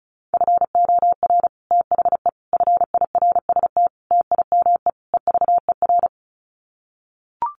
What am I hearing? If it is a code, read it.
Morse code, heard as a keyed tone, about 34 wpm, 700 Hz. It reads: FOR THE FIRST TIME EVER
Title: Sets of 5 Words Encore2 35wpm